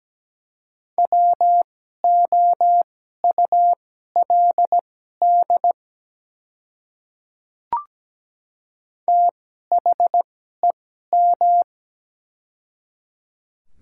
Morse code, heard as a keyed tone, about 17 wpm, spaced as for 12 wpm, 700 Hz. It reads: WOULD THEM
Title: 2x Character Spacing - Top 100 Words - 17wpm